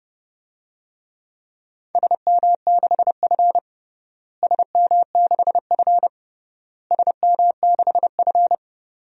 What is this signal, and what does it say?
Morse code, heard as a keyed tone, about 29 wpm, 700 Hz. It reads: SM6F SM6F SM6F
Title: DXCC-100 Call Signs - 4 Characters - Repeated 3x - 30wpm